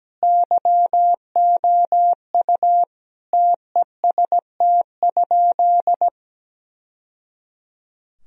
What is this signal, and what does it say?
Morse code, heard as a keyed tone, about 17 wpm, 700 Hz. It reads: YOU TEST?
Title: Sentences from Top 1000 Words - Review Part 3 - 17wpm